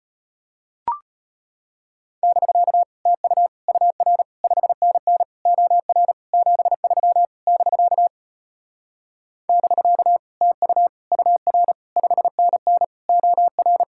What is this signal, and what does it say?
Morse code, heard as a keyed tone, about 33 wpm, 700 Hz. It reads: <BK> TU UR 5NN OR 73 <BK> <BK> TU UR 5NN OR